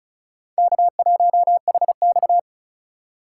Morse code, heard as a keyed tone, about 34 wpm, 700 Hz. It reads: K1HX
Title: USA Call Signs - 4 Characters - 3x Character Spacing - 35wpm